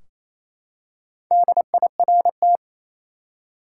Morse code, heard as a keyed tone, about 28 wpm, 700 Hz. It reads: DIRT
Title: Four Letter Words - 28wpm